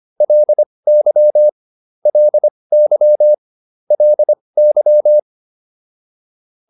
Morse code, heard as a keyed tone, about 25 wpm, spaced as for 20 wpm, 600 Hz. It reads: LY LY LY